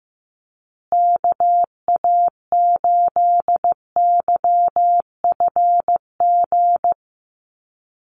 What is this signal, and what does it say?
Morse code, heard as a keyed tone, about 15 wpm, 700 Hz. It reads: KA8YFG
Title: USA Call Signs Repeated 2x 15wpm